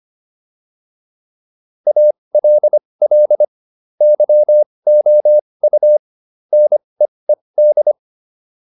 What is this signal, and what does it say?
Morse code, heard as a keyed tone, about 25 wpm, spaced as for 20 wpm, 600 Hz. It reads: ALL YOU NEED